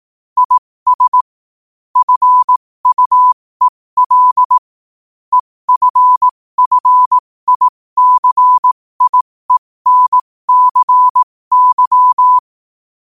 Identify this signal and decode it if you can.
Morse code, heard as a keyed tone, about 18 wpm, spaced as for 15 wpm, 1000 Hz. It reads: IS FUEL EFFICIENCY
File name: qotdfiles / 15wpm